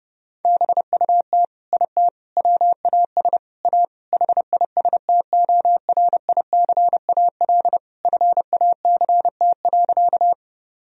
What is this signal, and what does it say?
Morse code, heard as a keyed tone, about 32 wpm, 700 Hz. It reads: BUT IT WAS A HISTORICAL FACT.
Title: Sentences from Top 1000 Words - Review Part 4 - 30wpm